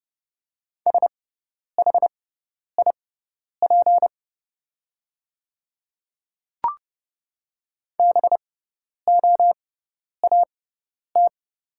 Morse code, heard as a keyed tone, about 30 wpm, spaced as for 10 wpm, 700 Hz. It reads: SHIP BOAT
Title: 6x Character Spacing - Top 500 Words - 30wpm